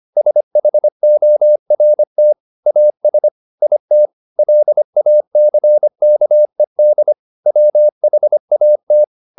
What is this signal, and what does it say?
Morse code, heard as a keyed tone, about 25 wpm, 600 Hz. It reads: SHORT AS IT LACKED WHAT